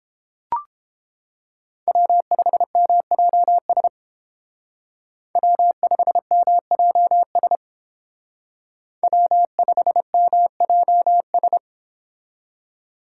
Morse code, heard as a keyed tone, about 31 wpm, 700 Hz. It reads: W5MJS W5MJS W5MJS